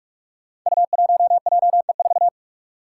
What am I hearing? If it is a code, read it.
Morse code, heard as a keyed tone, about 46 wpm, 700 Hz. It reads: U1JE4
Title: Five Challenging Letter-Number Combinations 45wpm